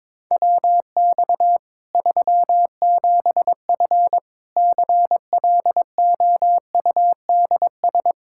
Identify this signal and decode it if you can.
Morse code, heard as a keyed tone, about 22 wpm, 700 Hz. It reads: WX 37F CLOUDS
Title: Practice QSO - Exchange 2 - Info - 22wpm